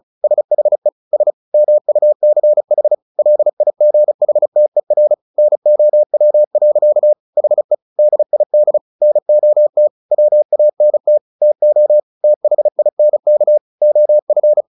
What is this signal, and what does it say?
Morse code, heard as a keyed tone, about 34 wpm, 600 Hz. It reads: SHE S MUCH LIGHTER NOW. HE DID NOT WANT TO THINK OF
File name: the old man and the sea 2 35wpm